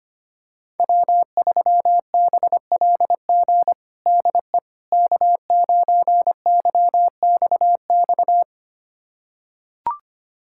Morse code, heard as a keyed tone, about 25 wpm, 700 Hz. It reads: W3BLG DE K9YXX
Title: USA Callsign DE Callsign - 5 Characters - 25wpm